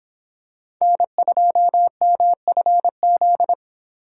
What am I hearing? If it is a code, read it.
Morse code, heard as a keyed tone, about 25 wpm, 700 Hz. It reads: N2MFZ